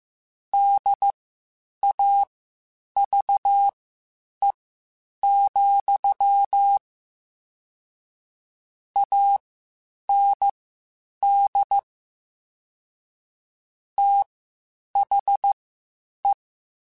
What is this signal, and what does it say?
Morse code, heard as a keyed tone, about 15 wpm, spaced as for 8 wpm, 800 Hz. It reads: DAVE, AND THE
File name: (spaced) / QSOs Spaced-mp3s / WA7QOX 2 @13 sp